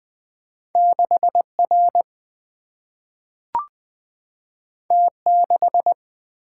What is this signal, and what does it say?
Morse code, heard as a keyed tone, about 20 wpm, 700 Hz. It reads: 6R T6
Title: Single-Letter Single-Number 20wpm